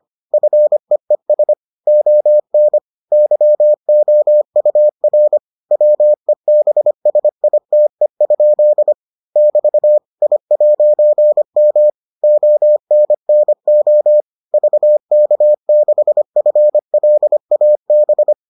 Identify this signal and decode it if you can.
Morse code, heard as a keyed tone, about 25 wpm, 600 Hz. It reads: FEES ON YOUR WEBSITE? <BT> I'M ONNO VK6FLAB